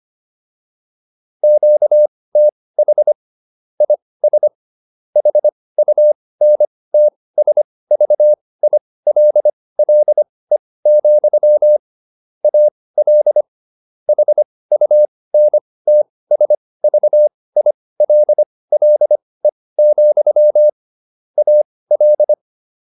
Morse code, heard as a keyed tone, about 25 wpm, spaced as for 18 wpm, 600 Hz. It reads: QTH IS HUNTSVILLE, AL HUNTSVILLE, AL